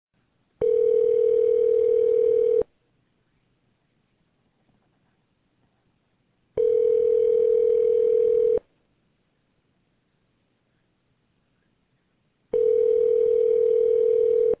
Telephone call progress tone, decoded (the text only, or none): ringback tone